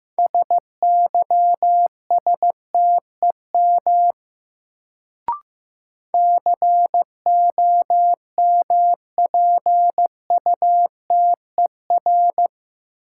Morse code, heard as a keyed tone, about 15 wpm, 700 Hz. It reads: SYSTEM COMPUTER
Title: Sentences from Top 1000 Words - Review Part 3 - New Word Drill - 15wpm